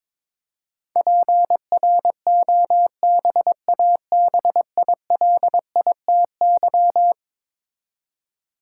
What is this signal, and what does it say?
Morse code, heard as a keyed tone, about 22 wpm, 700 Hz. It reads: PROBABILITY
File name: practice / 22wpm / Eleven Letter Words - 22wpm